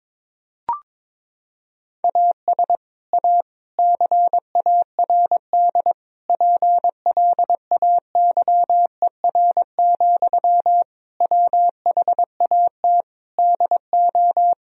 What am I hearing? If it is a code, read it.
Morse code, heard as a keyed tone, about 22 wpm, 700 Hz. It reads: AS A CARD PLAYER, WHAT DO
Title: Sentences from Top 1000 Words - Review Part 4 - 22wpm